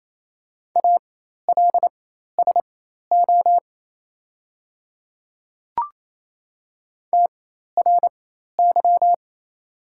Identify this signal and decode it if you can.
Morse code, heard as a keyed tone, about 27 wpm, spaced as for 13 wpm, 700 Hz. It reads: ALSO TRY
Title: 4x Character Spacing - Top 500 Words - 28wpm